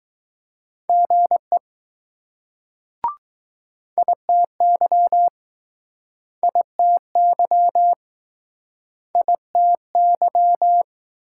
Morse code, heard as a keyed tone, about 20 wpm, 700 Hz. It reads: GE ITY ITY ITY